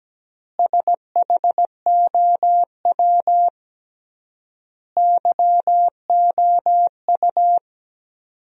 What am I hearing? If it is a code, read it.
Morse code, heard as a keyed tone, about 17 wpm, 700 Hz. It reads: SHOW YOU
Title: 3x Word-Spacing - Sentences from Top 500 Words - 17wpm